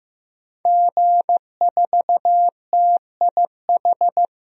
Morse code, heard as a keyed tone, about 15 wpm, 700 Hz. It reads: G4TIH
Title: DXCC-100 Call Signs - 5 Characters - Repeated 2x - 15wpm